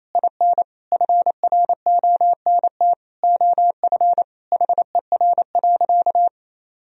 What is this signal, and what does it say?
Morse code, heard as a keyed tone, about 28 wpm, 700 Hz. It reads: IN FRONT OF HER.
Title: Sentences from Top 1000 Words - Review Part 2 - 28wpm